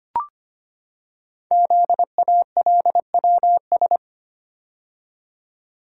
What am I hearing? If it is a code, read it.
Morse code, heard as a keyed tone, about 25 wpm, 700 Hz. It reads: ZALWS
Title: Code Groups - 5 Letters - 25wpm